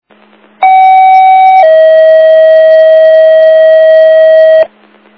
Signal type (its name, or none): two-tone page